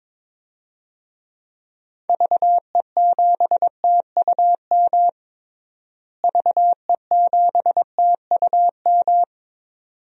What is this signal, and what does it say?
Morse code, heard as a keyed tone, about 22 wpm, 700 Hz. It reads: VE7TUM VE7TUM
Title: DXCC-100 Call Signs - 6 Characters - Repeated 2x - 22wpm